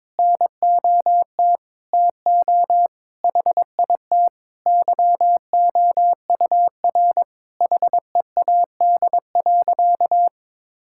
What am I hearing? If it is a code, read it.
Morse code, heard as a keyed tone, about 22 wpm, 700 Hz. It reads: NOT TO HIT YOUR HEAD.
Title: Sentences from Top 1600 Words 22wpm